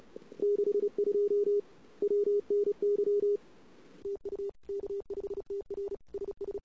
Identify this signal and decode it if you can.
Morse code, heard as a keyed tone, about 36 wpm, 410 Hz. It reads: 62 WNY TU K5TR SS
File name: CW / 2008-ss-cw-k5tr-op-n5rz -21:00